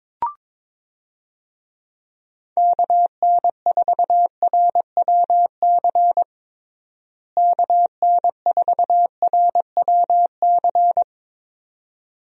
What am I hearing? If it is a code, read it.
Morse code, heard as a keyed tone, about 22 wpm, 700 Hz. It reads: KN4RWC KN4RWC
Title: USA Call Signs - 6 Characters - Repeated 2x - 22wpm